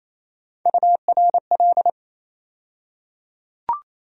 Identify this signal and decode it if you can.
Morse code, heard as a keyed tone, about 28 wpm, 700 Hz. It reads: URL
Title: USA Call Sign Postfixes - 3 Letters 28wpm